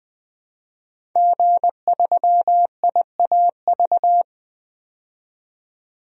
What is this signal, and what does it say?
Morse code, heard as a keyed tone, about 20 wpm, 700 Hz. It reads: G3IAV
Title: DXCC-100 Call Signs - 5 Characters - 20wpm